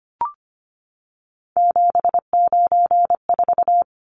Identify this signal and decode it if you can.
Morse code, heard as a keyed tone, about 25 wpm, 700 Hz. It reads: 794
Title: Numbers - 3-Digit 25wpm